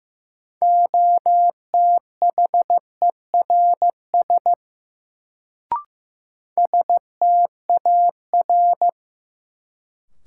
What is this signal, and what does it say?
Morse code, heard as a keyed tone, about 15 wpm, 700 Hz. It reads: OTHERS STAR
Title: Sentences from Top 1000 Words - Review Part 2 - New Word Drill - 15wpm